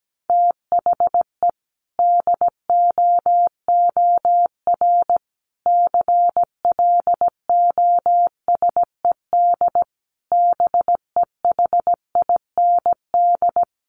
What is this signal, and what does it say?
Morse code, heard as a keyed tone, about 17 wpm, 700 Hz. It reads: THE DOOR CLOSED BEHIND
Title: Sentences from Top 1400 Words 17wpm